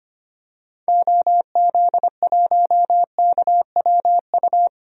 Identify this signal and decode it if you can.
Morse code, heard as a keyed tone, about 25 wpm, 700 Hz. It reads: OZ1KWU
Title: DXCC-100 Call Signs - 6 Characters - 25wpm